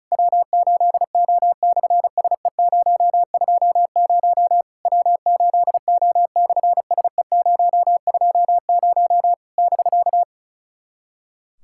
Morse code, heard as a keyed tone, about 36 wpm, 700 Hz. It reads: W8O/SE020 W8O/SE020 <BK>
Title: SOTA QSO Practice 35wpm